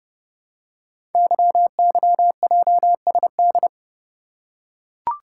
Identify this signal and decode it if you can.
Morse code, heard as a keyed tone, about 29 wpm, 700 Hz. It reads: YYJSD
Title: Code Groups - 5 Letters - 30wpm